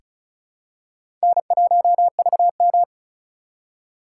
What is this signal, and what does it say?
Morse code, heard as a keyed tone, about 36 wpm, 700 Hz. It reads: N1VM
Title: USA Call Signs - 4x Character Spacing - 35wpm